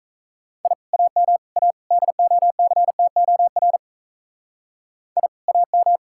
Morse code, heard as a keyed tone, about 44 wpm, 700 Hz. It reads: I AM A DOCTOR I AM